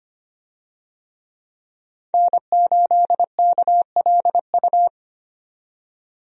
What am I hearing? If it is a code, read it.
Morse code, heard as a keyed tone, about 25 wpm, 700 Hz. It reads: N8KLU
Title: USA Call Signs - 5 Characters - 25wpm